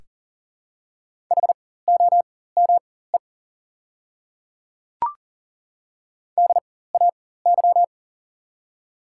Morse code, heard as a keyed tone, about 40 wpm, spaced as for 19 wpm, 700 Hz. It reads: HOME DAY
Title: 4x Character Spacing - Top 300 Words - 40wpm